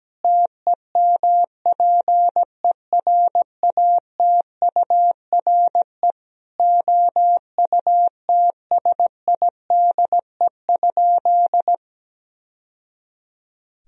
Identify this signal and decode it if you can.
Morse code, heard as a keyed tone, about 17 wpm, 700 Hz. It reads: TEMPERATURE OUTSIDE?
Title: Sentences from Top 1000 Words - Review Part 4 - 17wpm